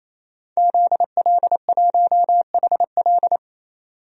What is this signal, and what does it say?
Morse code, heard as a keyed tone, about 28 wpm, 700 Hz. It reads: ZL1HL
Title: DXCC-100 Call Signs - 5 Characters - 28wpm